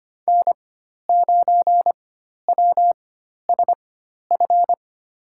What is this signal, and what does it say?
Morse code, heard as a keyed tone, about 25 wpm, spaced as for 12 wpm, 700 Hz. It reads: N9WSF
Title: USA Call Signs - 5 Characters - 4x Character Spacing - 25wpm